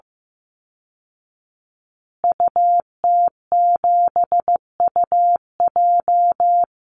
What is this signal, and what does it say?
Morse code, heard as a keyed tone, about 15 wpm, 700 Hz. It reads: UT7UJ
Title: Call Signs Repeated 2x - 15wpm